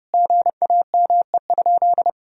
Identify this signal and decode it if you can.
Morse code, heard as a keyed tone, about 30 wpm, 700 Hz. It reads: GAME?
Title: Sentences from Top 1000 Words - Review Part 4 - 30wpm